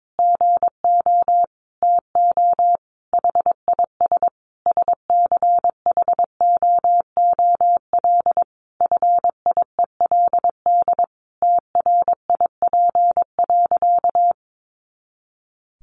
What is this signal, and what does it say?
Morse code, heard as a keyed tone, about 22 wpm, 700 Hz. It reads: GO TO HIS SCHOOL FIELD TRIP.
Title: Additional Easy Phrases 22wpm